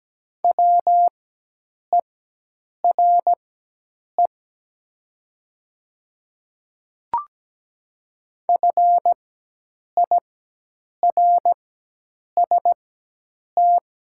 Morse code, heard as a keyed tone, about 17 wpm, spaced as for 8 wpm, 700 Hz. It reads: WERE FIRST
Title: 4x Character Spacing - Top 500 Words - 17wpm